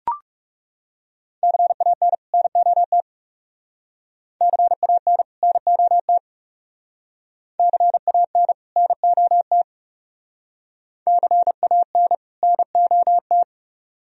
Morse code, heard as a keyed tone, about 37 wpm, 700 Hz. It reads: CAN NOT CAN NOT CAN NOT CAN NOT
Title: Speed-Racing - Sets of 2 Words 30wpm